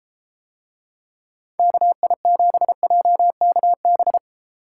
Morse code, heard as a keyed tone, about 32 wpm, 700 Hz. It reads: KI7JKB